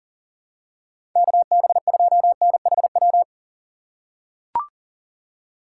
Morse code, heard as a keyed tone, about 40 wpm, 700 Hz. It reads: KB2NHW
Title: USA Call Signs - 6 Characters - 3x Character Spacing - 40wpm